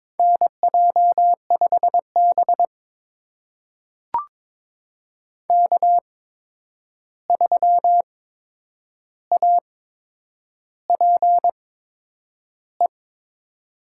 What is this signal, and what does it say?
Morse code, heard as a keyed tone, about 22 wpm, spaced as for 9 wpm, 700 Hz. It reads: NJ5B K3APE
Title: USA Call Signs - 8x Character Spacing - 22wpm